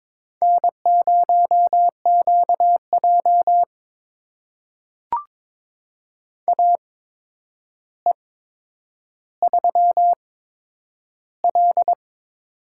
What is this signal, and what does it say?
Morse code, heard as a keyed tone, about 23 wpm, spaced as for 10 wpm, 700 Hz. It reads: N0QJ AE3L
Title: USA Call Signs - 4 Characters - 8x Character Spacing - 22wpm